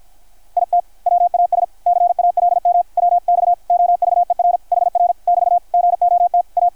Morse code, heard as a keyed tone, about 51 wpm, 660 Hz. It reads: IT WAS CALM WX OVER HR <BT> GOT A
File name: archived / CRSnet